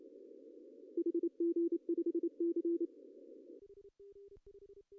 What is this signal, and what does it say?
Morse code, heard as a keyed tone, about 28 wpm, 340 Hz. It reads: HG5C
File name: CW / cqww-cw-k5tr